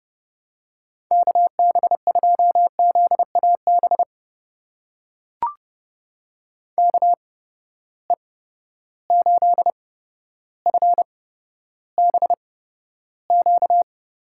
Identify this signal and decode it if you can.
Morse code, heard as a keyed tone, about 30 wpm, spaced as for 13 wpm, 700 Hz. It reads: KB2ZAB KE8FBQ
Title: USA Call Signs - 6 Characters - 8x Character Spacing - 30wpm